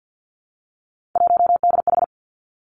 Morse code, heard as a keyed tone, about 50 wpm, 700 Hz. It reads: 1DH
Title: Three Challenging Letter-Number Combinations 50wpm